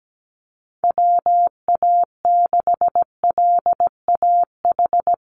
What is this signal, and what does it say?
Morse code, heard as a keyed tone, about 17 wpm, 700 Hz. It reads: WA6LAH
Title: USA Call Signs 17wpm